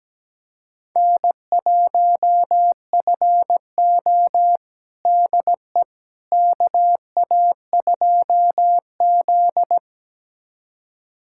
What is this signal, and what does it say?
Morse code, heard as a keyed tone, about 17 wpm, 700 Hz. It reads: N1FO DE KA2Z
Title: USA Callsign DE Callsign - 4 Characters - 17wpm